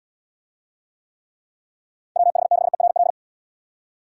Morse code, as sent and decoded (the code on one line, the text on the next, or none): .-- .... -.... .-. .-..
WH6ERL